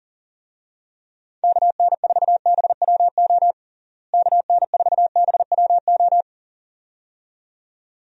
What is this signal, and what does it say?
Morse code, heard as a keyed tone, about 40 wpm, 700 Hz. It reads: KN4BWO KN4BWO
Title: USA Call Signs - 6 Characters - Repeated 2x - 40wpm